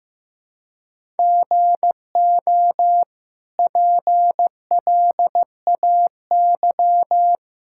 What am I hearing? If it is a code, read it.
Morse code, heard as a keyed tone, about 15 wpm, 700 Hz. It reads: GO PLAY
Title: Sentences from Top 1000 Words - Review Part 5 - 15wpm